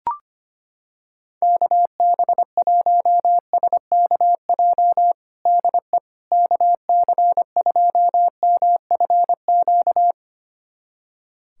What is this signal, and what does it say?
Morse code, heard as a keyed tone, about 25 wpm, 700 Hz. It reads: KB1SKJ DE KC2MFQ
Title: USA Callsign DE Callsign 25wpm